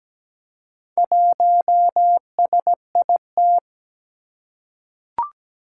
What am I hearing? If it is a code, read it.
Morse code, heard as a keyed tone, about 17 wpm, 700 Hz. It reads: 1SIT
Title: Four Challenging Letter-Number Combinations 17wpm